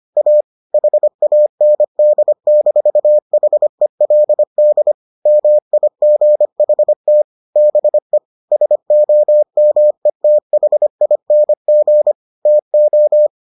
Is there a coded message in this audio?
Yes: Morse code from 0.2 to 13.3 s, heard as a keyed tone, about 25 wpm, 600 Hz. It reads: A HAND-HELD MIGHT BE SOMETHING TO